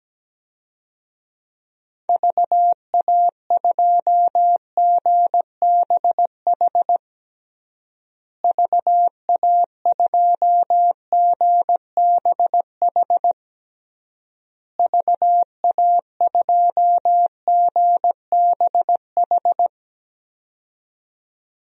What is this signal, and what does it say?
Morse code, heard as a keyed tone, about 17 wpm, 700 Hz. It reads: VA2GBH VA2GBH VA2GBH
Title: DXCC-100 Call Signs - 6 Characters - Repeated 3x - 17wpm